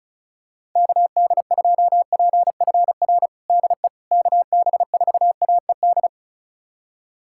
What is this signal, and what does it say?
Morse code, heard as a keyed tone, about 34 wpm, 700 Hz. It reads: KD2PFR DE KB4AED
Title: USA Callsign DE Callsign - 6 Characters - 35wpm